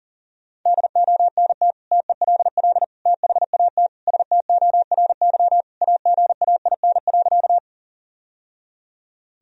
Morse code, heard as a keyed tone, about 37 wpm, 700 Hz. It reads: DONT TELL THAT STORY AGAIN.
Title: Sentences from Top 1000 Words - Review Part 1 - 40wpm